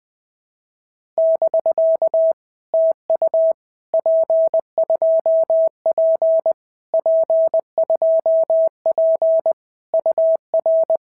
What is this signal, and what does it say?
Morse code, heard as a keyed tone, about 20 wpm, 650 Hz. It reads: <BK> TU P2P P2P UR
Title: POTA QSO Practice - 20wpm